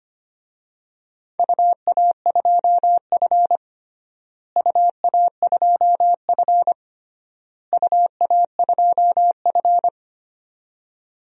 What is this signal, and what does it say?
Morse code, heard as a keyed tone, about 25 wpm, 700 Hz. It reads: UA2F UA2F UA2F
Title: DXCC-100 Call Signs - 4 Characters - Repeated 3x - 25wpm